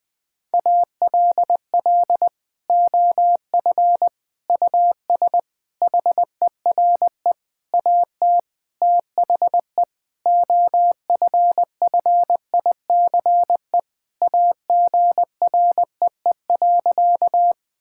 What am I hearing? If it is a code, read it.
Morse code, heard as a keyed tone, about 20 wpm, 700 Hz. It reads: ALL OF US HERE AT THE OFFICE AGREE.
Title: Sentences with 5-Word Phrases - 20wpm